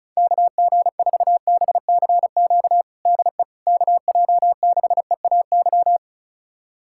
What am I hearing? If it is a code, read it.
Morse code, heard as a keyed tone, about 34 wpm, 700 Hz. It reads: KG4BCQ DE KJ6EAY
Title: USA Callsign DE Callsign - 6 Characters - 35wpm